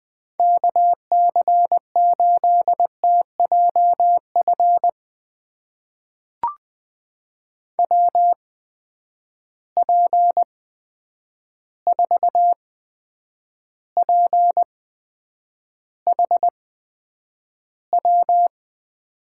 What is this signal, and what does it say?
Morse code, heard as a keyed tone, about 20 wpm, spaced as for 9 wpm, 700 Hz. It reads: KC8TJF WP4PHW
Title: USA Call Signs - 6 Characters - 8x Character Spacing - 20wpm